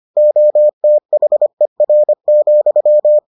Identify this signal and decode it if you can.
Morse code, heard as a keyed tone, about 25 wpm, 600 Hz. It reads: OTHER,